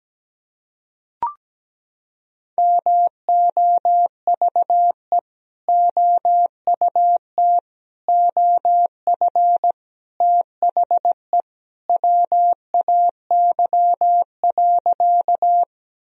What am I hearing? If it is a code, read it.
Morse code, heard as a keyed tone, about 17 wpm, 700 Hz. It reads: MOVE OUT OF THE WAY.
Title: Sentences from Top 1000 Words - Review Part 1 - 17wpm